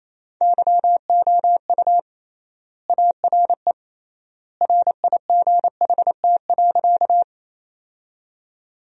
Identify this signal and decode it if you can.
Morse code, heard as a keyed tone, about 28 wpm, 700 Hz. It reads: YOU ARE RIGHT.
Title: 3x Word-Spacing - Sentences from Top 200 Words - 28wpm